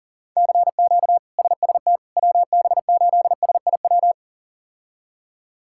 Morse code, heard as a keyed tone, about 40 wpm, 700 Hz. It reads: CQ SST WB8SIW
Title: K1USN SST QSO Practice - 40wpm